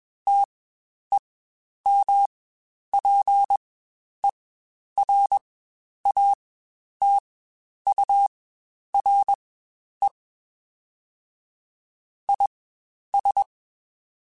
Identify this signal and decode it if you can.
Morse code, heard as a keyed tone, about 21 wpm, spaced as for 10 wpm, 800 Hz. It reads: TEMPERATURE IS
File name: Spaced Characters / Spaced Characters-mp3s / Weather 4 24 sp @20